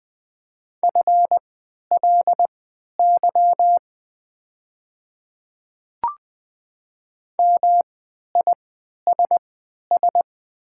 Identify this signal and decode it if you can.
Morse code, heard as a keyed tone, about 20 wpm, spaced as for 11 wpm, 700 Hz. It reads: FLY MISS
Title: 3x Character Spacing - Top 500 Words - 20wpm